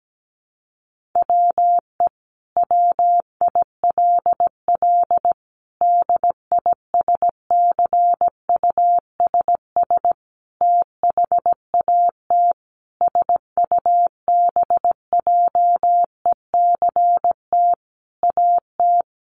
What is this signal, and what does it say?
Morse code, heard as a keyed tone, about 17 wpm, 700 Hz. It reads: WE WILL DISCUSS THAT SUBJECT AT
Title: Sentences from Top 1000 Words 17wpm